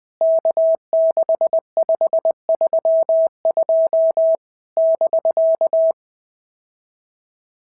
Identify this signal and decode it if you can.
Morse code, heard as a keyed tone, about 19 wpm, 650 Hz. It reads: K6532 <BK>
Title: POTA QSO Practice - 20wpm